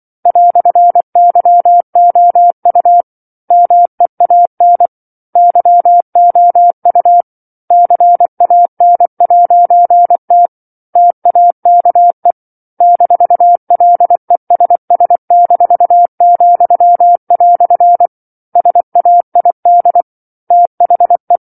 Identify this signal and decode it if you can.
Morse code, heard as a keyed tone, about 24 wpm, 700 Hz. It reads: "YOU MEAN YOU CAN'T TAKE -LESS-," SAID THE
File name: 07-Chapter-07 24 WPM